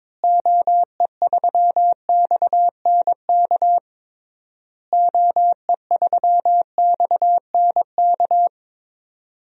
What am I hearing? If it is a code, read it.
Morse code, heard as a keyed tone, about 21 wpm, 700 Hz. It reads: OE3XNK OE3XNK
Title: DXCC-100 Call Signs - 6 Characters - Repeated 3x - 22wpm